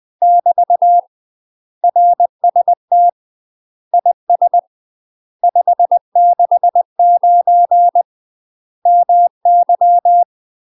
Morse code, heard as a keyed tone, about 20 wpm, 700 Hz. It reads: <BT> RST IS 569 MY
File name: UK MASTER QSOs TRAINING / 20WPM / UK MASTER QSOs 0042